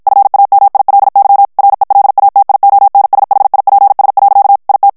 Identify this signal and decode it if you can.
Morse code, heard as a keyed tone, about 56 wpm, 800 Hz. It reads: FAMILY RELATIONSHIPS. IT
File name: morse mp3 / ANGLAIS texte clair 1 vitesse 50 mots minute